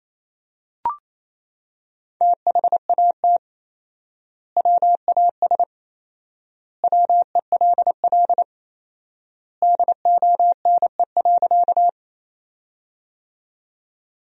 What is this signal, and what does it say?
Morse code, heard as a keyed tone, about 28 wpm, 700 Hz. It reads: THAT WAS WELL DONE.
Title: 4x Word-Spacing - Sentences from Top 500 Words - 28wpm